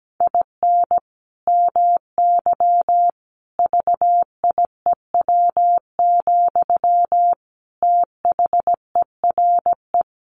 Morse code, heard as a keyed tone, about 17 wpm, 700 Hz. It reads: IN MY VIEW, THERE
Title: Sentences from Top 1500 Words 17wpm